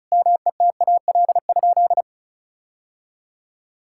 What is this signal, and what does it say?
Morse code, heard as a keyed tone, about 36 wpm, 700 Hz. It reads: METAL?
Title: Sentences from Top 1600 Words 35wpm